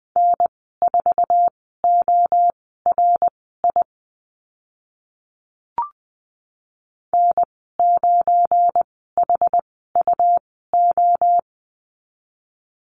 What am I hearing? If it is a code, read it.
Morse code, heard as a keyed tone, about 20 wpm, spaced as for 14 wpm, 700 Hz. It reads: N4ORI N9HUO
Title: 207 Lesson 33g - Letter j - Call Signs 20wpm